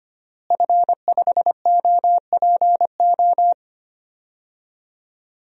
Morse code, heard as a keyed tone, about 25 wpm, 700 Hz. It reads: F5OPO
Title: DXCC-100 Call Signs - 5 Characters - Repeated 3x - 25wpm